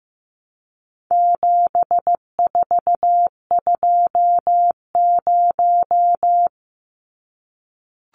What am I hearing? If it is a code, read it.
Morse code, heard as a keyed tone, about 15 wpm, 700 Hz. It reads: 7420
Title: Numbers - 4-Digit 15wpm